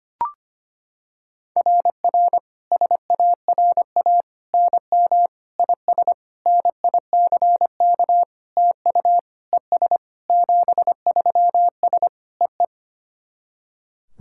Morse code, heard as a keyed tone, about 25 wpm, 700 Hz. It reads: RR SARA NM IS NICK TU ES 73S EE
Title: SOTA QSO Practice 25wpm